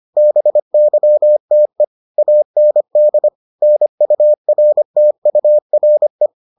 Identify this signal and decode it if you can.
Morse code, heard as a keyed tone, about 25 wpm, 600 Hz. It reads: BYTE AND NURTURE